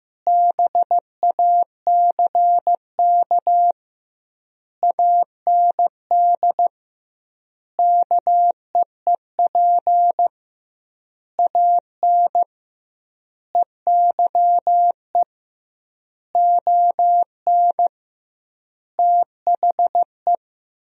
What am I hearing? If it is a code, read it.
Morse code, heard as a keyed tone, about 15 wpm, 700 Hz. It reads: BACK AND KEEP AN EYE ON THE
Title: 2x Word-Spacing - Sentences from Top 300 Words - 15wpm